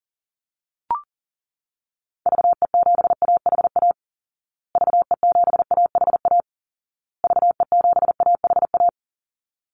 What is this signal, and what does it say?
Morse code, heard as a keyed tone, about 40 wpm, 700 Hz. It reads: VE7AHA VE7AHA VE7AHA
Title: Call Signs Repeated 2x - 40wpm